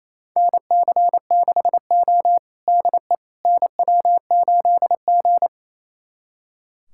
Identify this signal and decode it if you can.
Morse code, heard as a keyed tone, about 28 wpm, 690 Hz. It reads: NC6O DE NW8G
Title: USA Callsign DE Callsign - 4 Characters - 28wpm